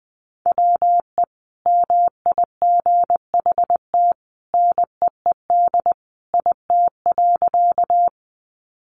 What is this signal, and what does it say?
Morse code, heard as a keyed tone, about 20 wpm, 700 Hz. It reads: WE MIGHT NEED IT.
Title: A Sound Of Thunder 20wpm